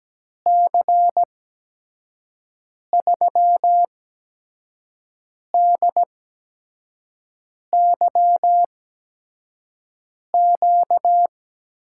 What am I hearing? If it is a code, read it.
Morse code, heard as a keyed tone, about 17 wpm, spaced as for 5 wpm, 700 Hz. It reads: C3DYQ
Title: USA Call Signs - 6 Characters - 8x Character Spacing - 17wpm